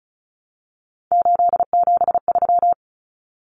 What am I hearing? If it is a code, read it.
Morse code, heard as a keyed tone, about 34 wpm, 700 Hz. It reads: 873
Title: Common Cut Numbers - 3-Digit 35wpm